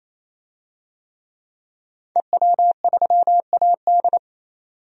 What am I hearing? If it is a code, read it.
Morse code, heard as a keyed tone, about 27 wpm, 700 Hz. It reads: EW3AD
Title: DXCC-100 Call Signs - 5 Characters - 28wpm